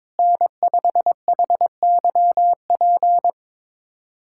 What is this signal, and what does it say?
Morse code, heard as a keyed tone, about 22 wpm, 700 Hz. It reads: N5HYP
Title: USA Callsign DE Callsign - 5 Characters - 22wpm